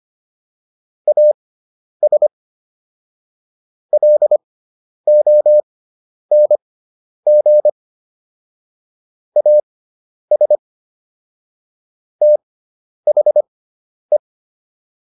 Morse code, heard as a keyed tone, about 25 wpm, spaced as for 10 wpm, 600 Hz. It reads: AS LONG AS THE
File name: WD304 10